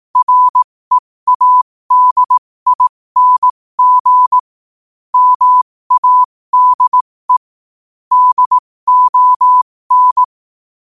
Morse code, heard as a keyed tone, about 18 wpm, spaced as for 15 wpm, 1000 Hz. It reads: READING MADE DON
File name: qotdfiles / 15wpm